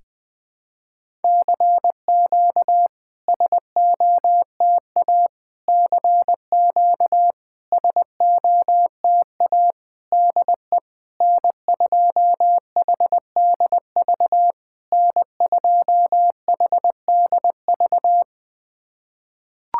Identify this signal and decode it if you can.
Morse code, heard as a keyed tone, about 20 wpm, 700 Hz. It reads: CQ SOTA CQ SOTA DE N2HDV N2HDV
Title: SOTA QSO Practice 20wpm